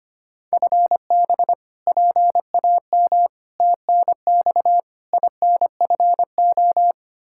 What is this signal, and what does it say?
Morse code, heard as a keyed tone, about 25 wpm, 700 Hz. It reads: FB PAM TNX INFO
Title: Practice QSO - Exchange 3 - More Info - 25wpm